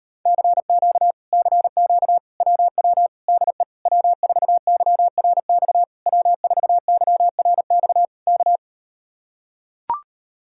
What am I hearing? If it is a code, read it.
Morse code, heard as a keyed tone, about 38 wpm, 700 Hz. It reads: CQ CQ WW DE W4YRX W4YRX K